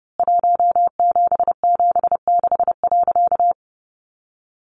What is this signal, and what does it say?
Morse code, heard as a keyed tone, about 28 wpm, 700 Hz. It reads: 1776.
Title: A Sound Of Thunder 30wpm